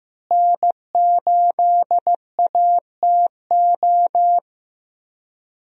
Morse code, heard as a keyed tone, about 15 wpm, 700 Hz. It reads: N8ATO
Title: USA Call Signs - 5 Characters - Repeated 3x - 15wpm